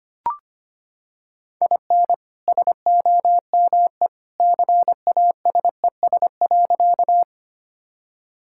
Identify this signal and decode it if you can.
Morse code, heard as a keyed tone, about 25 wpm, 700 Hz. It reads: IN SOME CASES.
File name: practice / 25wpm / Sentences with 2-Word Phrases - 25wpm